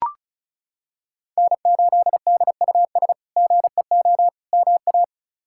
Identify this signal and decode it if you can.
Morse code, heard as a keyed tone, about 34 wpm, 700 Hz. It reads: N8DUS GEO MA
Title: K1USN SST QSO Practice - 35wpm